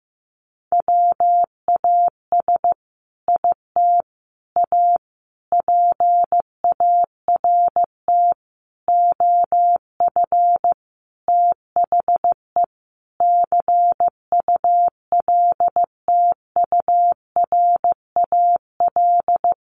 Morse code, heard as a keyed tone, about 15 wpm, 700 Hz. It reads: WAS IT A PART OF THE CULTURAL
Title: Sentences from Top 1500 Words 15wpm